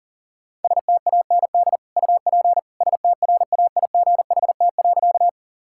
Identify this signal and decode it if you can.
Morse code, heard as a keyed tone, about 40 wpm, 700 Hz. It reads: STAND UP STRAIGHT.
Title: Sentences from Top 1000 Words - Review Part 6 - 40wpm